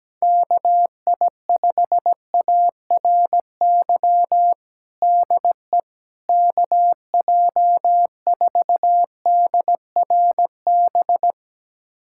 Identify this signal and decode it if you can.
Morse code, heard as a keyed tone, about 17 wpm, 700 Hz. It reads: KI5ARY DE KJ4DRB
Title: USA Callsign DE Callsign 17wpm